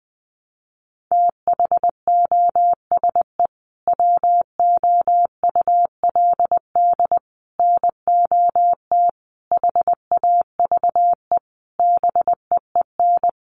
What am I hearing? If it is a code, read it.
Morse code, heard as a keyed tone, about 20 wpm, 700 Hz. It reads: THOSE WOULD NOT HAVE BEEN
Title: Sentences from Top 1300 Words 20wpm